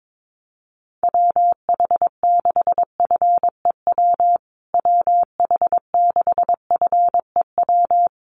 Morse code, heard as a keyed tone, about 22 wpm, 700 Hz. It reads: WH6FEW WH6FEW
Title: USA Call Signs Repeated 3x 22wpm